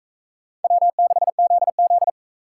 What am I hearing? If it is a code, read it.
Morse code, heard as a keyed tone, about 40 wpm, 700 Hz. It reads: W6ZZ